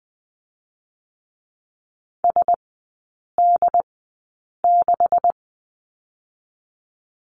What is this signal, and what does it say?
Morse code, heard as a keyed tone, about 20 wpm, spaced as for 8 wpm, 700 Hz. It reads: SD6
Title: 138 Lesson 23e - Letter f - Three Characters 20wpm